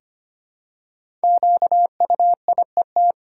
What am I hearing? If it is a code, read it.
Morse code, heard as a keyed tone, about 25 wpm, 700 Hz. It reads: QUIET
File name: 5 Letter Words - First 3 Letters Given - 25wpm